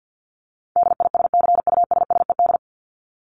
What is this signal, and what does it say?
Morse code, heard as a keyed tone, about 50 wpm, 700 Hz. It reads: DISCUSSED
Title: ED Suffix - Top 100 Words with Base Words 50wpm